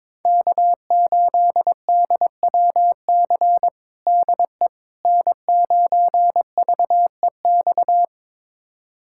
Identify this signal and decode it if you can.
Morse code, heard as a keyed tone, about 22 wpm, 700 Hz. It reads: K8DWC DE N9VEX
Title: USA Callsign DE Callsign - 5 Characters - 22wpm